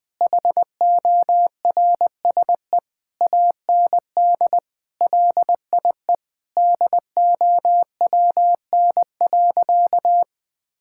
Morse code, heard as a keyed tone, about 20 wpm, 700 Hz. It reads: HORSE AND LIE DOWN.
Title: Sentences from Top 1600 Words 20wpm